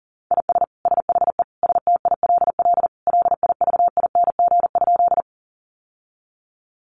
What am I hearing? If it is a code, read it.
Morse code, heard as a keyed tone, about 40 wpm, 700 Hz. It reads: IS SHE STILL LIVING?
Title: Sentences from Top 1100 Words 40wpm